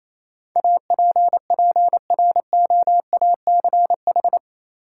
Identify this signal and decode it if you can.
Morse code, heard as a keyed tone, about 28 wpm, 700 Hz. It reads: APPROACH
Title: Sentences from Top 1400 Words - New Word Drill Alt - 28wpm